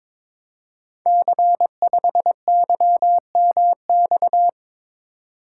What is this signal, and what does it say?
Morse code, heard as a keyed tone, about 21 wpm, 700 Hz. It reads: C5YMX
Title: Code Groups - 5 Letter:Numbers - 22wpm